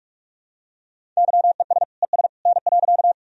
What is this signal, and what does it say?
Morse code, heard as a keyed tone, about 43 wpm, 700 Hz. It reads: YES ES N.
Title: Sentences from Top 500 Words and Abbreviations 45wpm